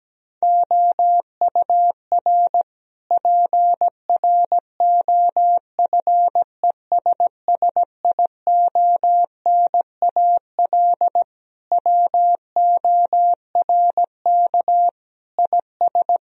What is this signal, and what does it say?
Morse code, heard as a keyed tone, about 17 wpm, 700 Hz. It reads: OUR PROFESSIONAL WORK IS
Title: Sentences from Top 1000 Words - Review Part 4 - 17wpm